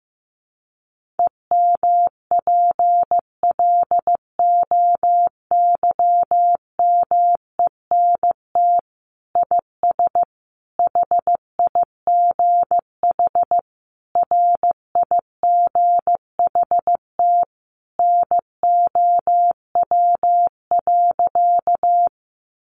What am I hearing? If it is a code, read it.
Morse code, heard as a keyed tone, about 15 wpm, 700 Hz. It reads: EMPLOYMENT IS HIGH RIGHT NOW.
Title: Sentences from Top 1500 Words 15wpm